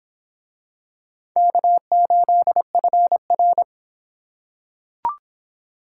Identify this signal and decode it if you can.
Morse code, heard as a keyed tone, about 24 wpm, 700 Hz. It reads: K8FR